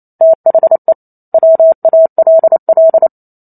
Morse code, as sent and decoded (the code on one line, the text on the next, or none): - .... / .-- .- .-.. .-..
THE WALL